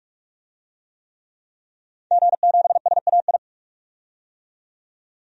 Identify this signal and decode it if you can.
Morse code, heard as a keyed tone, about 42 wpm, 700 Hz. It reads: G7SAI
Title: DXCC-100 Call Signs - 5 Characters - 45wpm